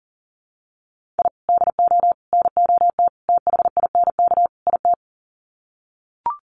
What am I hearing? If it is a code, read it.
Morse code, heard as a keyed tone, about 40 wpm, 700 Hz. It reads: I DO NOT THINK IT
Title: Sets of 5 Words 40wpm